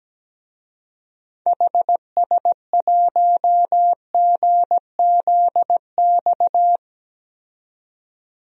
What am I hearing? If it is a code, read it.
Morse code, heard as a keyed tone, about 17 wpm, 700 Hz. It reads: HS1GZX
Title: DXCC-100 Call Signs - 6 Characters - 17wpm